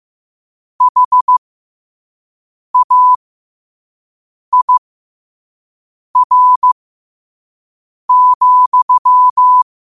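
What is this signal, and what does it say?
Morse code, heard as a keyed tone, about 15 wpm, spaced as for 5 wpm, 1000 Hz. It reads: HAIR,
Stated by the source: Morse code sent at 5 words per minute.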